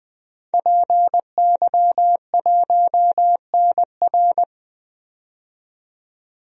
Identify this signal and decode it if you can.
Morse code, heard as a keyed tone, about 20 wpm, 700 Hz. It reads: PY1NR
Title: DXCC-100 Call Signs - 5 Characters - 20wpm